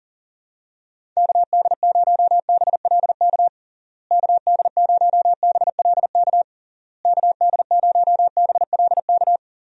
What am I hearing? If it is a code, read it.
Morse code, heard as a keyed tone, about 37 wpm, 700 Hz. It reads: KD0BLK KD0BLK KD0BLK
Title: USA Call Signs - 6 Characters - Repeated 3x - 40wpm